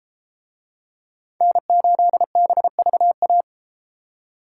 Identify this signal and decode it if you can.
Morse code, heard as a keyed tone, about 31 wpm, 700 Hz. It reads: N8BVA